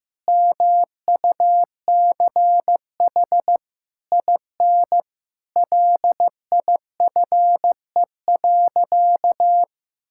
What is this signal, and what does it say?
Morse code, heard as a keyed tone, about 15 wpm, 700 Hz. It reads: MUCH IN LIFE.
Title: Sentences from Top 1000 Words - Review Part 6 - 15wpm